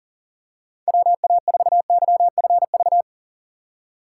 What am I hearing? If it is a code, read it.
Morse code, heard as a keyed tone, about 40 wpm, 700 Hz. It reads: WA4YFV
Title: USA Call Signs - 6 Characters - 2x Character Spacing - 40wpm